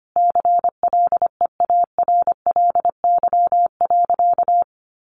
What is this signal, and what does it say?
Morse code, heard as a keyed tone, about 25 wpm, 700 Hz. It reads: CLEARLY.
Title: Sentences from Top 1300 Words 25wpm